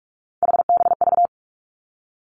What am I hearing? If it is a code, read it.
Morse code, heard as a keyed tone, about 46 wpm, 700 Hz. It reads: HBV
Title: Three Challenging Letter-Number Combinations 45wpm